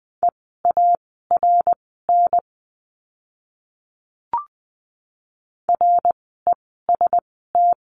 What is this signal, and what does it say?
Morse code, heard as a keyed tone, about 20 wpm, spaced as for 14 wpm, 700 Hz. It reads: EARN REST
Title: 054 Lesson 10f - Letter r - Words 20wpm